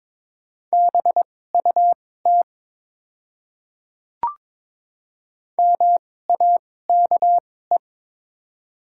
Morse code, heard as a keyed tone, about 22 wpm, spaced as for 16 wpm, 700 Hz. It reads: BUT MAKE
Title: 2x Character Spacing - Top 200 Words - 22wpm